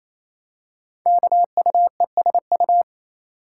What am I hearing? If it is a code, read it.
Morse code, heard as a keyed tone, about 28 wpm, 700 Hz. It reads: KUESU
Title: Code Groups - 5 Letter:Numbers - 28wpm